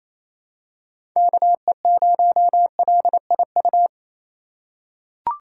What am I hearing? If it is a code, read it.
Morse code, heard as a keyed tone, about 28 wpm, 700 Hz. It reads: KE0LIU
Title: USA Call Signs - 6 Characters - 3x Character Spacing - 28wpm